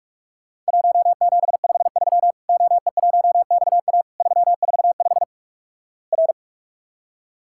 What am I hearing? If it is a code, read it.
Morse code, heard as a keyed tone, about 44 wpm, 700 Hz. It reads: 1753 OE1XA 345 R
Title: Worked All Europe DX Contest - QTC Exchange - 45wpm